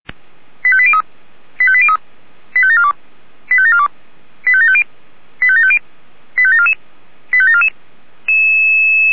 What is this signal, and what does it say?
Five-tone selective call (ZVEI), heard as quick runs of five tones, tones of about 70 ms. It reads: 85803 85803 85742 85742 85750 85750 85744 85744